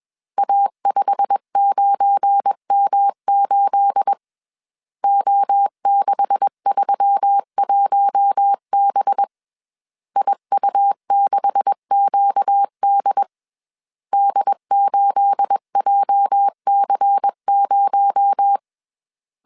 Morse code, heard as a keyed tone, about 21 wpm, 800 Hz. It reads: A59M8 O631B IU6QD D8JC0
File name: CW / DET / 1 9 1500C S C 20wpm 2310